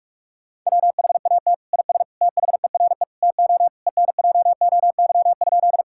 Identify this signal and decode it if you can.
Morse code, heard as a keyed tone, about 46 wpm, 700 Hz. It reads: WHAT IS THERE TO ENJOY?
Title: Sentences from Top 1000 Words - Review Part 5 - 45wpm